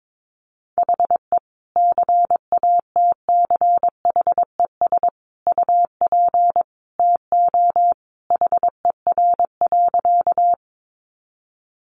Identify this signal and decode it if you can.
Morse code, heard as a keyed tone, about 22 wpm, 700 Hz. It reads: HE CATCHES UP TO HER.
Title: Additional Easy Phrases 22wpm